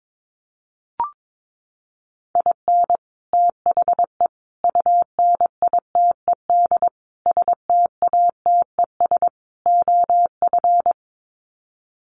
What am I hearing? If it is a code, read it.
Morse code, heard as a keyed tone, about 22 wpm, 700 Hz. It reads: IN THE UNITED STATES OF
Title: Sets of 5 Words Encore2 22wpm